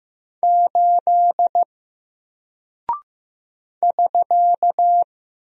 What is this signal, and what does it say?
Morse code, heard as a keyed tone, about 15 wpm, 700 Hz. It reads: 8 <SK>
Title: Single Character (Letter:Number:Punctuation:Prosign) Alt 15wpm